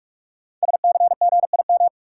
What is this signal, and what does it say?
Morse code, heard as a keyed tone, about 44 wpm, 700 Hz. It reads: SCGIM
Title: Code Groups - 5 Letter:Numbers - 45wpm